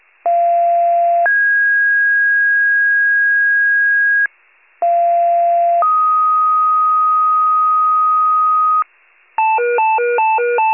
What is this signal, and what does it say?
Two-tone page, heard as a one-second tone, then a three-second tone, about 690 then 1650 Hz, about 690 then 1250 Hz, with noise. It is heard from 0.2 to 8.8 s.
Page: 41 RESCUE DUTY